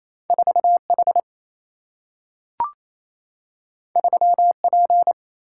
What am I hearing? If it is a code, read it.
Morse code, heard as a keyed tone, about 28 wpm, 700 Hz. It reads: 4H 3P
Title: Single-Letter Single-Number 28wpm